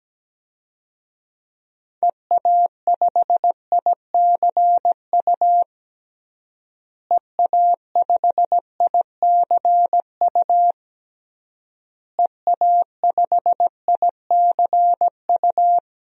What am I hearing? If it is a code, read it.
Morse code, heard as a keyed tone, about 17 wpm, 700 Hz. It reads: EA5ICU EA5ICU EA5ICU
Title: DXCC-100 Call Signs - 6 Characters - Repeated 3x - 17wpm